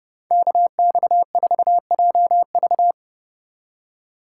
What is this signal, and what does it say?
Morse code, heard as a keyed tone, about 33 wpm, 700 Hz. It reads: KX4JV